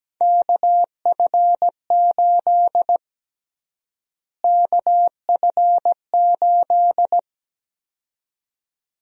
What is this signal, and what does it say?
Morse code, heard as a keyed tone, about 17 wpm, 700 Hz. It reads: KF8 KF8
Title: USA Call Sign Prefixes - Repeated 2x - 17wpm